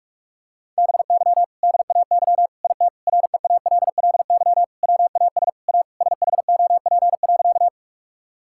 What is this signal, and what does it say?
Morse code, heard as a keyed tone, about 43 wpm, 700 Hz. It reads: BY DAY IT REALLY WAS A SHOP.
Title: Sentences from Top 1600 Words 45wpm